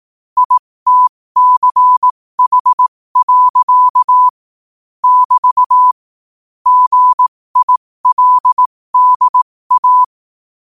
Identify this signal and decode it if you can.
Morse code, heard as a keyed tone, about 18 wpm, spaced as for 15 wpm, 1000 Hz. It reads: ITCH. <BT> GILDA
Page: New quotes every day in morse code at 15 Words per minute.